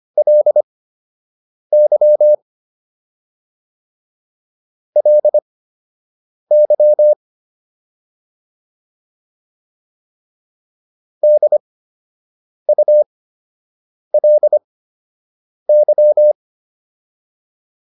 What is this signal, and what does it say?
Morse code, heard as a keyed tone, about 25 wpm, spaced as for 7 wpm, 600 Hz. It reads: LY LY DULY